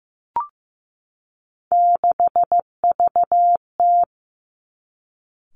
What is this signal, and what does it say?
Morse code, heard as a keyed tone, about 15 wpm, 700 Hz. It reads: 6VT
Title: Three Challenging Letter-Number Combinations 15wpm